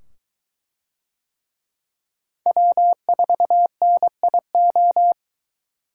Morse code, heard as a keyed tone, about 23 wpm, 700 Hz. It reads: W4NIO